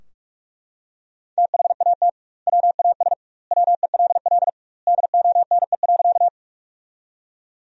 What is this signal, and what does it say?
Morse code, heard as a keyed tone, about 41 wpm, 700 Hz. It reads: THAT WAS WELL DONE.
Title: 2x Word-Spacing - Sentences from Top 500 Words - 45wpm